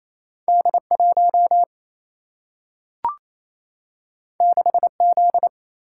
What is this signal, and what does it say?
Morse code, heard as a keyed tone, about 28 wpm, 700 Hz. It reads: D1 6Z
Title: Single-Letter Single-Number 28wpm